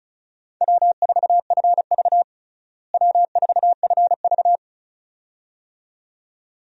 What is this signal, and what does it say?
Morse code, heard as a keyed tone, about 36 wpm, 700 Hz. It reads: W4FV W4FV
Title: USA Call Signs - 4 Characters - Repeated 2x - 35wpm